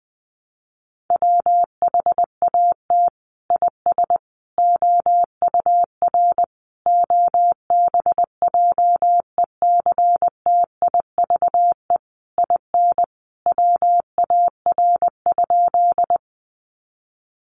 Morse code, heard as a keyed tone, about 20 wpm, 700 Hz. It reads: WHAT IS OUR OBJECTIVE IN WAR?
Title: Sentences from Top 1200 Words 20wpm